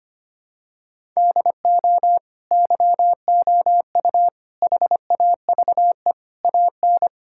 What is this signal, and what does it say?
Morse code, heard as a keyed tone, about 25 wpm, 700 Hz. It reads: DO YOU HAVE AN
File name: practice / 25wpm / Sentences with 9-Word Phrases - 25wpm